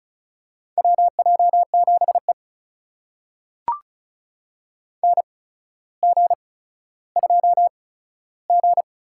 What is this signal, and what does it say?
Morse code, heard as a keyed tone, about 36 wpm, spaced as for 15 wpm, 700 Hz. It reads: WJ7E NG2G
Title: USA Call Signs - 4 Characters - 8x Character Spacing - 35wpm